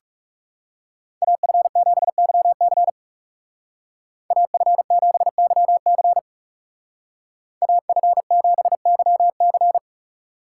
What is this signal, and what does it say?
Morse code, heard as a keyed tone, about 39 wpm, 700 Hz. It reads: AF7YC AF7YC AF7YC